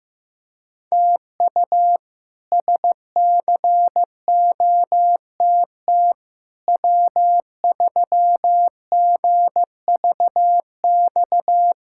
Morse code, heard as a keyed tone, about 16 wpm, 700 Hz. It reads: TU SCOTT W3GVX
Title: K1USN SST QSO Practice - 15wpm